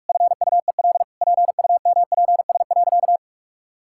Morse code, heard as a keyed tone, about 42 wpm, 700 Hz. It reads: FUEL PUMPS.
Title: Pangram Challenge v2 - 45wpm